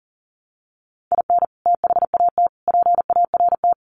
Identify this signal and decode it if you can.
Morse code, heard as a keyed tone, about 40 wpm, 700 Hz. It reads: IN THAT PART
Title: Sets of 5 Words Encore2 40wpm